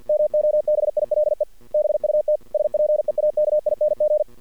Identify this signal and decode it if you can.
Morse code, heard as a keyed tone, about 48 wpm, 610 Hz. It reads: MOBILE BUT SPEAKING
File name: archived / CRSnet